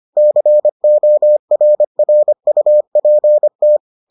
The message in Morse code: CORRUPT

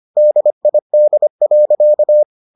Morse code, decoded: DID.

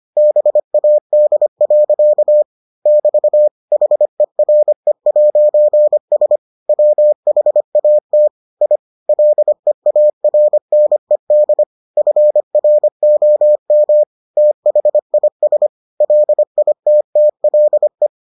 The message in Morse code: BAD. <BT> HERE'S WHAT I LEARNED FROM THIS LITTLE

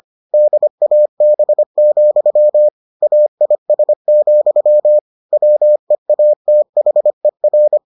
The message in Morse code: DAB, AIS, WEATHER

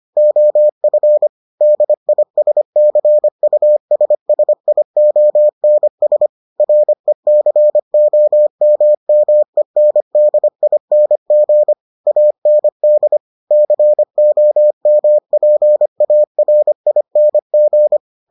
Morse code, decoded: OF DISCUSSIONS RECOMMENDING AND COMPARING

